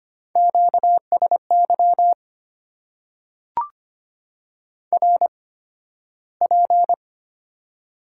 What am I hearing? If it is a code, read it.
Morse code, heard as a keyed tone, about 25 wpm, spaced as for 13 wpm, 700 Hz. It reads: QSY RP
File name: Top 90 QSO Elements - 8x Character Spacing - 25wpm